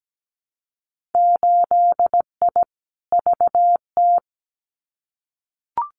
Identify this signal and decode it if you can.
Morse code, heard as a keyed tone, about 17 wpm, 700 Hz. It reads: 8I VT
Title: Winter Field Day Exchanges Encore 17wpm